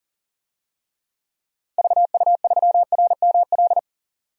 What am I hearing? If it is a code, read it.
Morse code, heard as a keyed tone, about 38 wpm, 700 Hz. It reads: VU3RML
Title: DXCC-100 Call Signs - 6 Characters - 40wpm